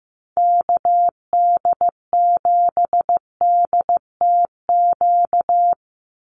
Morse code, heard as a keyed tone, about 15 wpm, 700 Hz. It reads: KD7DTQ
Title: USA Call Signs 15wpm